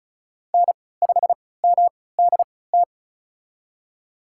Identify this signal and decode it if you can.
Morse code, heard as a keyed tone, about 34 wpm, spaced as for 20 wpm, 700 Hz. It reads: N5MDT
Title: USA Call Signs - 5 Characters - 3x Character Spacing - 35wpm